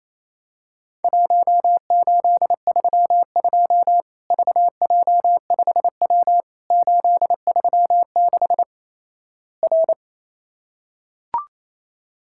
Morse code, heard as a keyed tone, about 28 wpm, 700 Hz. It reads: 1832 VJ5W 836 R
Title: Worked All Europe DX Contest - QTC Exchange - 28wpm